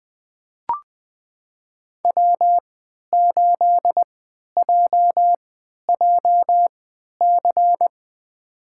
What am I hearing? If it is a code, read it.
Morse code, heard as a keyed tone, about 20 wpm, spaced as for 11 wpm, 700 Hz. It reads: W8JJC
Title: USA Call Signs - 5 Characters - 3x Character Spacing - 20wpm